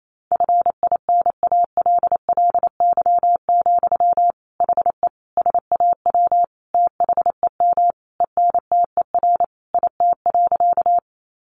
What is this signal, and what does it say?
Morse code, heard as a keyed tone, about 28 wpm, 700 Hz. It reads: FINALLY, HE SAW THEM ENTER IT.
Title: Sentences from Top 1300 Words 28wpm